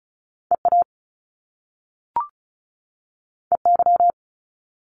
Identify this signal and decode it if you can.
Morse code, heard as a keyed tone, about 36 wpm, 700 Hz. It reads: EA EY
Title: Top Two Letter Combinations in Words 35wpm